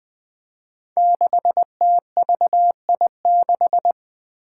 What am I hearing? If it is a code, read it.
Morse code, heard as a keyed tone, about 20 wpm, 700 Hz. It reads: 6TVI6
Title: Five Challenging Letter-Number Combinations 20wpm